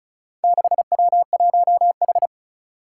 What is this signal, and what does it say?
Morse code, heard as a keyed tone, about 36 wpm, 700 Hz. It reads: 6W1H
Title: Four Challenging Letter-Number Combinations 35wpm